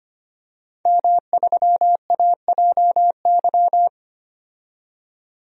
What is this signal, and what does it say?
Morse code, heard as a keyed tone, about 25 wpm, 700 Hz. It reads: M3AJY
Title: DXCC-100 Call Signs - 5 Characters - 25wpm